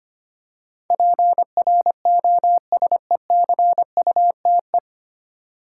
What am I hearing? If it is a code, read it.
Morse code, heard as a keyed tone, about 25 wpm, 700 Hz. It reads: PROSECUTE
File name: Nine Letter Words - 25wpm